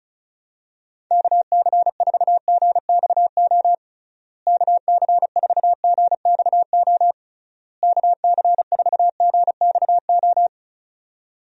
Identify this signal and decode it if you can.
Morse code, heard as a keyed tone, about 36 wpm, 700 Hz. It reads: KC4GXO KC4GXO KC4GXO
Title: USA Call Signs - 6 Characters - Repeated 3x - 35wpm